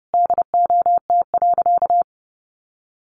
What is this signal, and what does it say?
Morse code, heard as a keyed tone, about 28 wpm, 700 Hz. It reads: DOT.
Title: Sentences from Top 1500 Words 30wpm